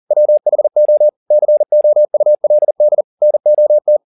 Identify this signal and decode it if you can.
Morse code, heard as a keyed tone, about 40 wpm, 600 Hz. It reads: WHO COULD NOT